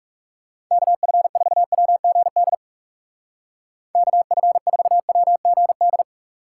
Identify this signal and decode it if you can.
Morse code, heard as a keyed tone, about 40 wpm, 700 Hz. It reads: KF4WGD KF4WGD